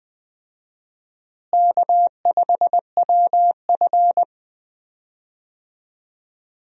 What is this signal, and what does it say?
Morse code, heard as a keyed tone, about 21 wpm, 700 Hz. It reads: K5WF
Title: USA Call Signs - 4 Characters - Repeated 3x - 20wpm